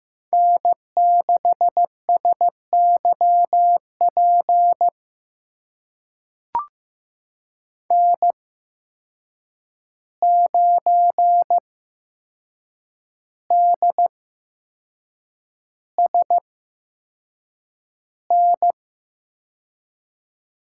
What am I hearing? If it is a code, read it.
Morse code, heard as a keyed tone, about 15 wpm, spaced as for 6 wpm, 700 Hz. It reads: N6SYP N9DSN
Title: USA Call Signs - 8x Character Spacing - 15wpm